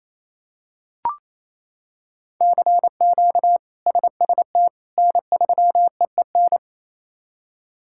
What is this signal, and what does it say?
Morse code, heard as a keyed tone, about 28 wpm, 700 Hz. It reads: CQ SST N3EEN
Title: K1USN SST QSO Practice - 28wpm